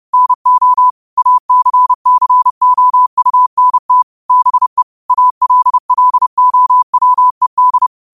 Morse code, heard as a keyed tone, about 30 wpm, 1000 Hz. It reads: NO ACCOUNT BE ALLOWED